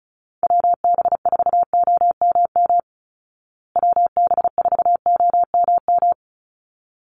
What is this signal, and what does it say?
Morse code, heard as a keyed tone, about 34 wpm, 700 Hz. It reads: WB4OMM WB4OMM
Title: Call Signs Repeated 3x - 35wpm